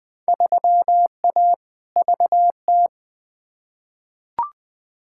Morse code, heard as a keyed tone, about 20 wpm, 700 Hz. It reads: 3A VT
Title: Field Day Exchanges Encore - 20wpm